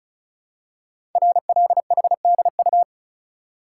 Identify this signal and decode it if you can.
Morse code, heard as a keyed tone, about 36 wpm, 690 Hz. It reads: RLHDU
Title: Code Groups - 5 Letters - 35wpm